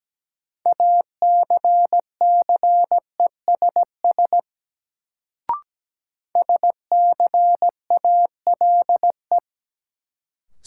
Morse code, heard as a keyed tone, about 17 wpm, 700 Hz. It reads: ACCESS SCALE
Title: Sentences from Top 1000 Words - Review Part 4 - New Word Drill - 17wpm